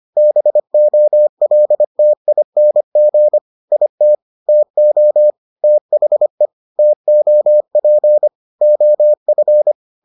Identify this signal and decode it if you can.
Morse code, heard as a keyed tone, about 25 wpm, 600 Hz. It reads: BOLTING IT TO THE TOP OF